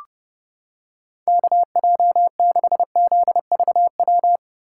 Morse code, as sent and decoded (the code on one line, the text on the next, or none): -.- .--- -.... --.. ...- .--
KJ6ZVW